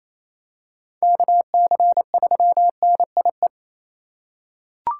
Morse code, heard as a keyed tone, about 27 wpm, 700 Hz. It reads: KC3NIE
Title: K1USN SST QSO Practice - 28wpm